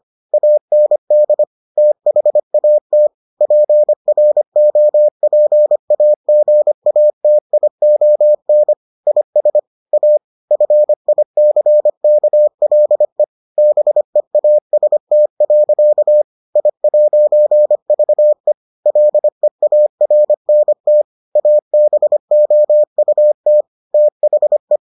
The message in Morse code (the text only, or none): AND THAT PROPAGATION IS A FICKLE BEAST. I'VE LEARNT ABOUT THE